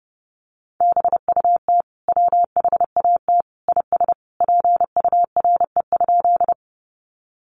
Morse code, heard as a keyed tone, about 30 wpm, 700 Hz. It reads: BUT WHAT IS PURE?
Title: Sentences from Top 1500 Words 30wpm